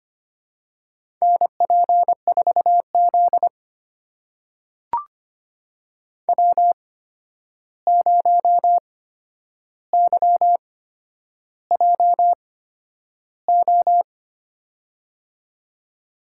Morse code, heard as a keyed tone, about 25 wpm, spaced as for 10 wpm, 700 Hz. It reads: NP4Z W0YJO
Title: USA Call Signs - 8x Character Spacing - 25wpm